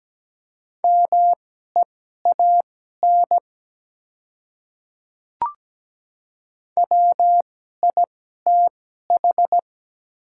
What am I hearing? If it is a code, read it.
Morse code, heard as a keyed tone, about 17 wpm, spaced as for 12 wpm, 700 Hz. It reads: MEAN WITH
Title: 2x Character Spacing - Top 300 Words - 17wpm